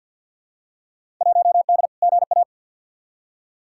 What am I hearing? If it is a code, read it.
Morse code, heard as a keyed tone, about 47 wpm, 700 Hz. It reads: 1D GA
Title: Field Day Exchanges - 50wpm